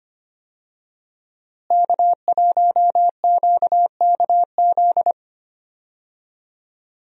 Morse code, heard as a keyed tone, about 25 wpm, 700 Hz. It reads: K1QKZ
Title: USA Call Signs - 5 Characters - Repeated 2x - 25wpm